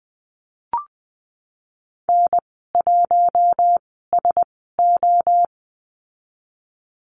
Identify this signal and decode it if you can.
Morse code, heard as a keyed tone, about 20 wpm, spaced as for 15 wpm, 700 Hz. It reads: N1SO
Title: 048 Lesson 9g - Number 4 - Call Signs 20wpm